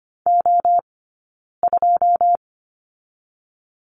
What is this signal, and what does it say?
Morse code, heard as a keyed tone, about 25 wpm, 700 Hz. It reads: O 2
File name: course / 25:10wpm / 099 Lesson 17d - Letter u - Two Characters 25wpm